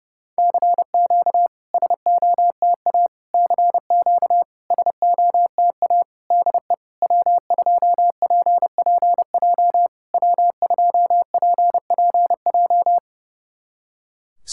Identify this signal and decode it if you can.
Morse code, heard as a keyed tone, about 30 wpm, 690 Hz. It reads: CQ SOTA CQ SOTA DE W2PPJ W2PPJ
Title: SOTA QSO Practice 30wpm